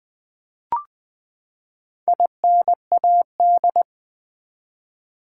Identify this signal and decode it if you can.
Morse code, heard as a keyed tone, about 20 wpm, 700 Hz. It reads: INAD
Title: Build A Word - 4 Letters At A Time - 10-Letter Words - 20wpm